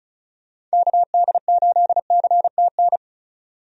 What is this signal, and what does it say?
Morse code, heard as a keyed tone, about 34 wpm, 700 Hz. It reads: KD8CTN
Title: USA Call Signs - 6 Characters - 35wpm